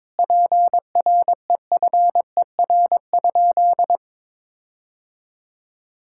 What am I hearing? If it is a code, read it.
Morse code, heard as a keyed tone, about 22 wpm, 700 Hz. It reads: PREFER?
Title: Sentences from Top 1600 Words 22wpm